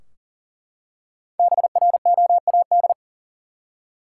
Morse code, heard as a keyed tone, about 40 wpm, 700 Hz. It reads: BROAD
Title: Five Letter Words - 40wpm